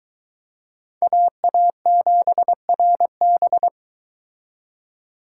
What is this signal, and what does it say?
Morse code, heard as a keyed tone, about 22 wpm, 700 Hz. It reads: AA7RB